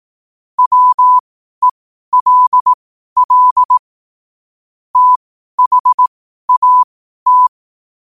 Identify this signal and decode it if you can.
Morse code, heard as a keyed tone, about 18 wpm, spaced as for 12 wpm, 1000 Hz. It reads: WELL THAT
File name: qotdfiles / 12wpm